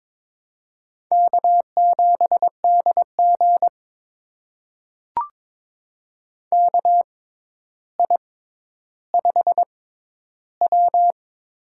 Morse code, heard as a keyed tone, about 21 wpm, spaced as for 11 wpm, 700 Hz. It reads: K7DG KI5W
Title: USA Call Signs - 4 Characters - 6x Character Spacing - 22wpm